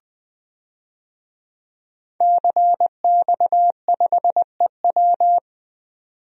Morse code, heard as a keyed tone, about 20 wpm, 700 Hz. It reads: CX5EW
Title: DXCC-100 Call Signs - 5 Characters - 20wpm